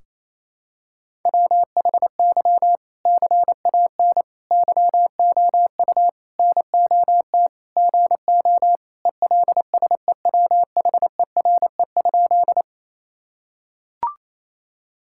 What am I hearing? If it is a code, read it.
Morse code, heard as a keyed tone, about 28 wpm, 700 Hz. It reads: WHY CAN YOU NOT GO ELSEWHERE?
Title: Sentences from Top 1600 Words 28wpm